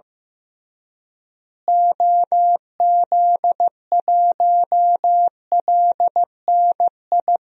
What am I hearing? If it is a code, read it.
Morse code, heard as a keyed tone, about 15 wpm, 700 Hz. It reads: OZ1LNI
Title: DXCC-100 Call Signs - 6 Characters - Repeated 3x - 15wpm